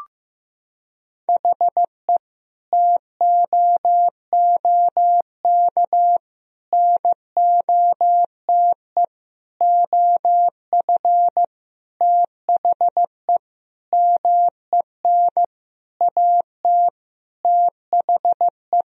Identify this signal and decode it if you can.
Morse code, heard as a keyed tone, about 16 wpm, 700 Hz. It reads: HE TOOK NOTE OF THE MEN AT THE
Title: Sentences from Top 1000 Words - Review Part 2 - 15wpm